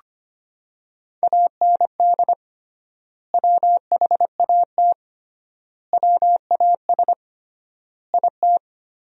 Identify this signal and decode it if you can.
Morse code, heard as a keyed tone, about 25 wpm, 700 Hz. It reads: AND WHAT WAS IT
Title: 3x Word-Spacing - Sentences from Top 100 Words - 25wpm